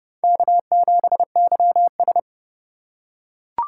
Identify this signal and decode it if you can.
Morse code, heard as a keyed tone, about 30 wpm, 700 Hz. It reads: K7YS
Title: USA Call Signs - 4 Characters - 4x Character Spacing - 30wpm